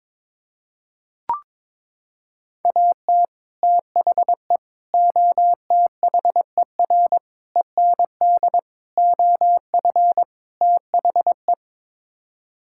Morse code, heard as a keyed tone, about 22 wpm, 700 Hz. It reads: AT THE OTHER END OF THE
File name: practice / 22wpm / Sets of 6 Words - 22wpm